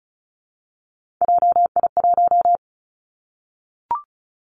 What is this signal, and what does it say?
Morse code, heard as a keyed tone, about 36 wpm, 700 Hz. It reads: JI1
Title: Three Challenging Letter-Number Combinations 35wpm